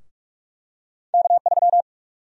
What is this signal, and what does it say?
Morse code, heard as a keyed tone, about 46 wpm, 700 Hz. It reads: K3
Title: USA Call Sign Prefixes - 45wpm